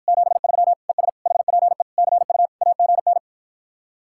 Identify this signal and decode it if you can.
Morse code, heard as a keyed tone, about 52 wpm, 700 Hz. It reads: 73 ES HPE CU AGN